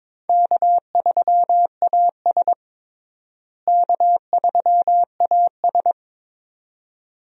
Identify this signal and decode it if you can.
Morse code, heard as a keyed tone, about 22 wpm, 700 Hz. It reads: K3AS K3AS
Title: USA Call Signs - 4 Characters - Repeated 2x - 22wpm